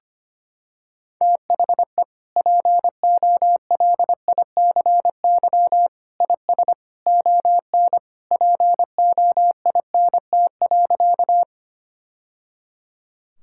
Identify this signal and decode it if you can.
Morse code, heard as a keyed tone, about 25 wpm, 700 Hz. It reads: THE POLICY IS ON POINT.
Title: Sentences from Top 1000 Words - Review Part 5 - 25wpm